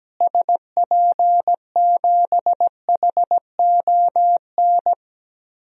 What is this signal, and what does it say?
Morse code, heard as a keyed tone, about 17 wpm, 700 Hz. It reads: SP7HON
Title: DXCC-100 Call Signs - 6 Characters - Repeated 2x - 17wpm